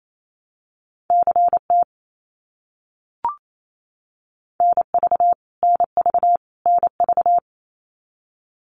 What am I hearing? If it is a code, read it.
Morse code, heard as a keyed tone, about 28 wpm, 700 Hz. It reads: CT NV NV NV
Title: US States Repeated 3x 28wpm